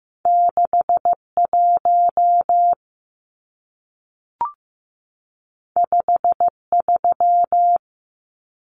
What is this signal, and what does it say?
Morse code, heard as a keyed tone, about 15 wpm, 700 Hz. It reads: 61 53
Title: Numbers - 2-Digit 15wpm